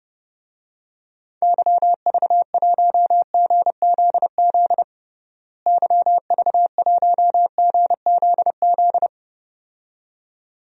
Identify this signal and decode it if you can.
Morse code, heard as a keyed tone, about 30 wpm, 700 Hz. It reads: YV1GZZ YV1GZZ
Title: DXCC-100 Call Signs - 6 Characters - Repeated 2x - 30wpm